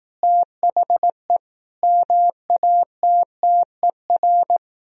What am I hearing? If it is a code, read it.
Morse code, heard as a keyed tone, about 18 wpm, 700 Hz. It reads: THE MATTER
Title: Speed-Racing - Sets of 5 Words 15wpm